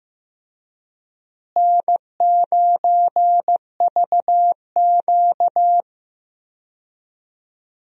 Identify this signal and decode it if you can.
Morse code, heard as a keyed tone, about 15 wpm, 700 Hz. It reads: N9VQ
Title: USA Call Signs - 4 Characters - Repeated 3x - 15wpm